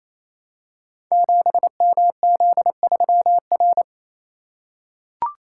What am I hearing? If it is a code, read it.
Morse code, heard as a keyed tone, about 27 wpm, 700 Hz. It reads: 7MZ3R
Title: Code Groups - 5 Letter:Numbers - 28wpm